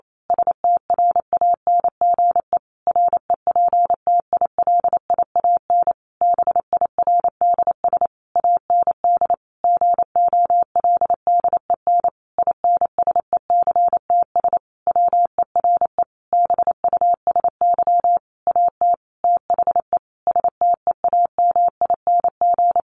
Morse code, heard as a keyed tone, about 28 wpm, 700 Hz. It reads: STRANGE REPTILIAN BIRDS AND GOLDEN INSECTS WERE BUSY AT THE STEAMING
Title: A Sound Of Thunder 28wpm